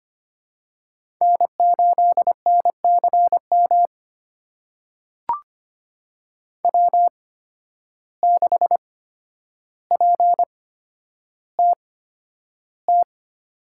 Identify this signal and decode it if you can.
Morse code, heard as a keyed tone, about 25 wpm, spaced as for 11 wpm, 700 Hz. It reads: N8NCM W6PTT
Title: USA Call Signs - 5 Characters - 8x Character Spacing - 25wpm